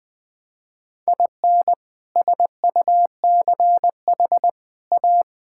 Morse code, heard as a keyed tone, about 20 wpm, 700 Hz. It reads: IN SUCH A
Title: Sets of 6 Words - 20wpm